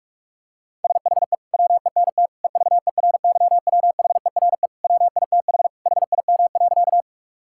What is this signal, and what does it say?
Morse code, heard as a keyed tone, about 41 wpm, 700 Hz. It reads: SHE WENT EVERYWHERE WITH HIM.
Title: Sentences from Top 1600 Words 45wpm